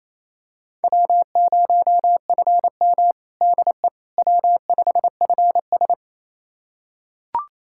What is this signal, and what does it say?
Morse code, heard as a keyed tone, about 27 wpm, 700 Hz. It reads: W0FM DE W5FS
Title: USA Callsign DE Callsign - 4 Characters - 28wpm